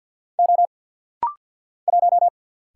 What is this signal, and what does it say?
Morse code, heard as a keyed tone, about 50 wpm, 700 Hz. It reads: O 1
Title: Single Letter-Number Alt - Rapid-Fire 50wpm